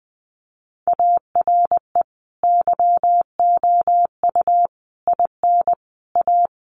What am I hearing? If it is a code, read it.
Morse code, heard as a keyed tone, about 20 wpm, 700 Hz. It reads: ARE YOU IN A
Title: Sentences from Top 1100 Words 20wpm